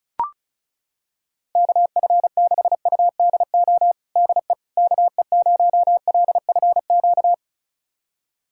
Morse code, heard as a keyed tone, about 34 wpm, 700 Hz. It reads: KF6UDO DE KE0LFQ
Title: USA Callsign DE Callsign 35wpm